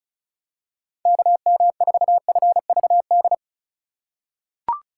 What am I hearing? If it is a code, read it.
Morse code, heard as a keyed tone, about 36 wpm, 700 Hz. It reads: KM4FVD
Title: USA Call Signs - 6 Characters - 3x Character Spacing - 35wpm